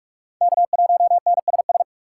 Morse code, heard as a keyed tone, about 40 wpm, 700 Hz. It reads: K1NSS